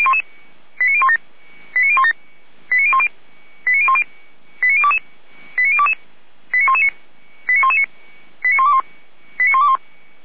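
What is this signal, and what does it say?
Five-tone selective call (ZVEI), heard as quick runs of five tones, tones of about 80 ms. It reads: 89017 89017 89010 89010 89022 89119 89119 89121 89121
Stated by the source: ZVEI five-tone selective call received by SDR.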